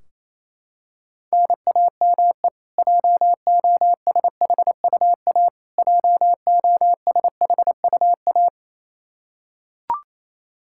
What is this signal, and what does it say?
Morse code, heard as a keyed tone, about 28 wpm, 700 Hz. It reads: NAME JOSHUA JOSHUA
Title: Practice QSO - Exchange 1 - Beginning - 28wpm